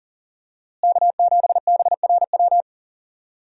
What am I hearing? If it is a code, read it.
Morse code, heard as a keyed tone, about 40 wpm, 700 Hz. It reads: K7BRW
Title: USA Call Signs - 5 Characters - Repeated 3x - 40wpm